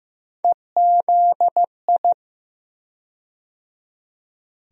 Morse code, heard as a keyed tone, about 15 wpm, 700 Hz. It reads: EZI
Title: USA Call Sign Postfixes - 3 Letters 15wpm